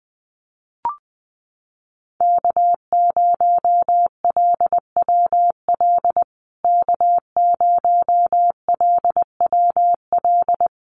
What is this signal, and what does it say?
Morse code, heard as a keyed tone, about 20 wpm, 700 Hz. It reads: K0LWL K0LWL
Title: USA Call Signs Repeated 3x 20wpm